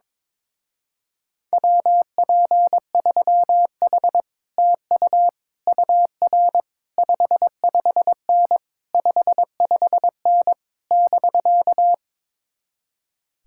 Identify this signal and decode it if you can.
Morse code, heard as a keyed tone, about 22 wpm, 700 Hz. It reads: WP3H TU UR 55N 55N <BK>
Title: Speed-Racing Common QSO Phrases - 22wpm